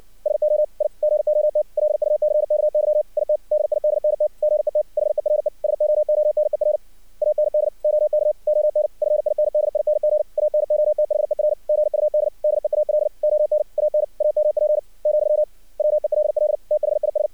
Morse code, heard as a keyed tone, about 62 wpm, 600 Hz. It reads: SO I GOT LUCKY IT DIDNT GET HERE SOONER AND OK ON FINDING ANOTHER GUD DEAL ON AN AMP , WELL THIS
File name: archived / CRSnet